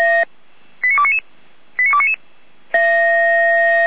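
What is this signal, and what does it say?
Five-tone selective call (ZVEI), heard as quick runs of five tones, tones of about 70 ms. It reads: 89299 89299
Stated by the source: ZVEI five-tone call received by SDR.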